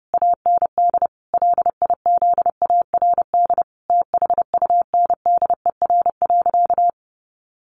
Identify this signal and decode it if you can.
Morse code, heard as a keyed tone, about 28 wpm, 700 Hz. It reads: AND LIZARD THUNDER.
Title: A Sound Of Thunder 30wpm